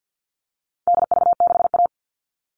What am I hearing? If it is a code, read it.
Morse code, heard as a keyed tone, about 50 wpm, 700 Hz. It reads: DV6A
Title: All Cut Numbers - 4-Digit 50wpm